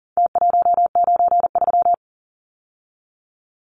Morse code, heard as a keyed tone, about 38 wpm, 700 Hz. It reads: T193
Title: Numbers - Contest Style - 4-Digit 40wpm